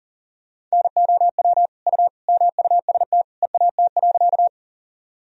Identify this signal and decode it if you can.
Morse code, heard as a keyed tone, about 40 wpm, 700 Hz. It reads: NOW U MUST EAT.
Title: Sentences from Top 300 Words and Abbreviations 40wpm